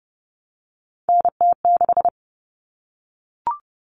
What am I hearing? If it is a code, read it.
Morse code, heard as a keyed tone, about 30 wpm, 700 Hz. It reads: NT6
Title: All Cut Numbers - 3-Digit 30wpm